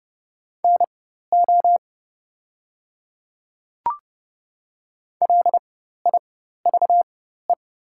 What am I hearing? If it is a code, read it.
Morse code, heard as a keyed tone, about 30 wpm, spaced as for 14 wpm, 700 Hz. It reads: NO LIVE
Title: 4x Character Spacing - Top 200 Words - 30wpm